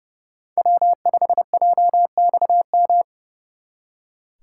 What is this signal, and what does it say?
Morse code, heard as a keyed tone, about 35 wpm, 700 Hz. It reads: W5JXM
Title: Speed Racing - USA 1x3 Call Signs 30wpm